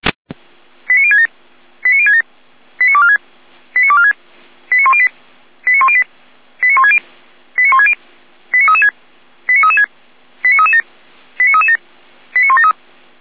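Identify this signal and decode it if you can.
Five-tone selective call (ZVEI), heard as quick runs of five tones, tones of about 70 ms. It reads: 89067 89067 89246 89246 89108 89108 89160 89160 89336 89336 89337 89337 89173
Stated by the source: ZVEI five-tone selective call 89560 received on SDR.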